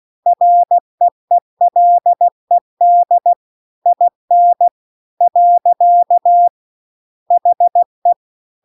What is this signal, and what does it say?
Morse code, heard as a keyed tone, about 16 wpm, 700 Hz. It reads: REELED IN . HE
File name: CW-The-Big-Twoheaded-River-2-16WPM